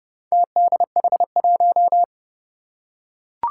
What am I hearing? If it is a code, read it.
Morse code, heard as a keyed tone, about 30 wpm, 700 Hz. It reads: TDH1
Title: Four Challenging Letter-Number Combinations 30wpm